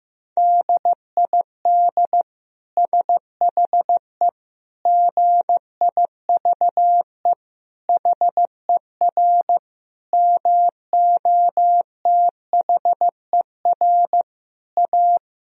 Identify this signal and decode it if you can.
Morse code, heard as a keyed tone, about 15 wpm, 700 Hz. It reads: DID SHE GIVE HER MOTHER A
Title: Sentences from Top 1000 Words - Review Part 4 - 15wpm